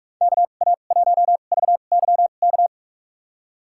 Morse code, heard as a keyed tone, about 45 wpm, spaced as for 32 wpm, 700 Hz. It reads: KA1VYK
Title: USA Call Signs - 2x Character Spacing - 45wpm